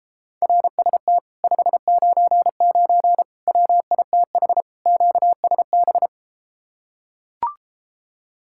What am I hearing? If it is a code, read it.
Morse code, heard as a keyed tone, about 33 wpm, 700 Hz. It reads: RST 599 WITH QSB